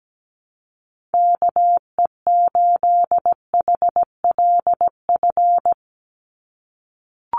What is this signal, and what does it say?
Morse code, heard as a keyed tone, about 17 wpm, 700 Hz. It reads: KE8HLF
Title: USA Call Signs 17wpm